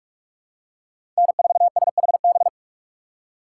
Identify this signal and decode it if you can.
Morse code, heard as a keyed tone, about 47 wpm, 690 Hz. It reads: N4SHB
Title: USA Call Signs - 5 Characters - 2x Character Spacing - 45wpm